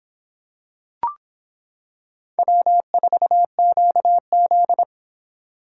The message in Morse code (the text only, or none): W4QZ